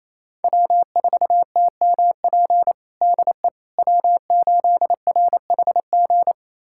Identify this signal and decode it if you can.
Morse code, heard as a keyed tone, about 27 wpm, 700 Hz. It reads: W4TMP DE W8RHG
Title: USA Callsign DE Callsign - 5 Characters - 28wpm